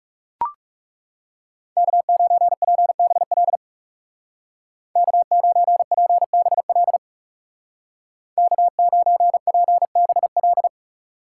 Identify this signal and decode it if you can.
Morse code, heard as a keyed tone, about 38 wpm, 700 Hz. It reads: K9PBL K9PBL K9PBL